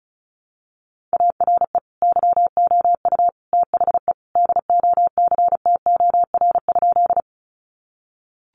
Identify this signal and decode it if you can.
Morse code, heard as a keyed tone, about 34 wpm, 700 Hz. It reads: ARE YOU THE DOCTOR?
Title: Sentences from Top 1500 Words 35wpm